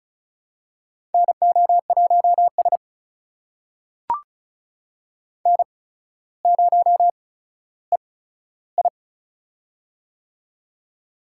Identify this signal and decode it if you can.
Morse code, heard as a keyed tone, about 36 wpm, spaced as for 15 wpm, 700 Hz. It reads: NO1S N0EI
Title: USA Call Signs - 4 Characters - 8x Character Spacing - 35wpm